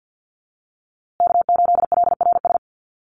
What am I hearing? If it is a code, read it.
Morse code, heard as a keyed tone, about 50 wpm, 700 Hz. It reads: K8LRS
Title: USA Call Signs Repeated 2x 50wpm